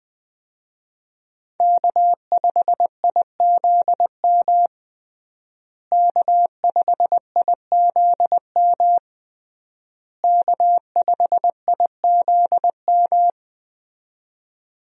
Morse code, heard as a keyed tone, about 20 wpm, 700 Hz. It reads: K5IZM K5IZM K5IZM
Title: USA Call Signs - 5 Characters - Repeated 3x - 20wpm